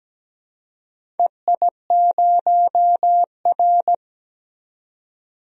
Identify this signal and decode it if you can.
Morse code, heard as a keyed tone, about 17 wpm, 700 Hz. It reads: EI0R
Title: DXCC-100 Call Signs - 4 Characters - 17wpm